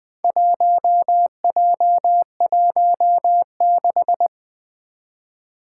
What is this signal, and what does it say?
Morse code, heard as a keyed tone, about 20 wpm, 700 Hz. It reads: 1J16
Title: Four Challenging Letter-Number Combinations 20wpm